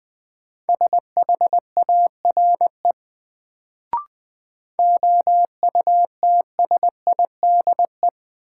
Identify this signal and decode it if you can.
Morse code, heard as a keyed tone, about 21 wpm, 700 Hz. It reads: SHARE OUTSIDE
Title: Sentences from Top 900 Words - New Word Drill - 20wpm